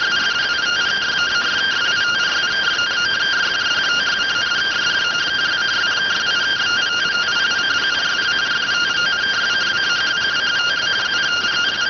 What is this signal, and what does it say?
Two-tone teleprinter signal, about 170 Hz shift.